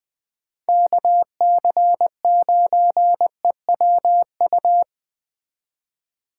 Morse code, heard as a keyed tone, about 20 wpm, 700 Hz. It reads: KC9EWU
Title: Speed Racing - USA Call Signs 17wpm